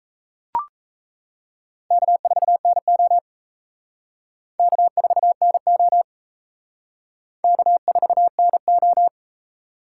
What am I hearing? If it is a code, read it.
Morse code, heard as a keyed tone, about 34 wpm, 700 Hz. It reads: K4NO K4NO K4NO
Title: Speed Racing - USA 1x2 Call Signs 28wpm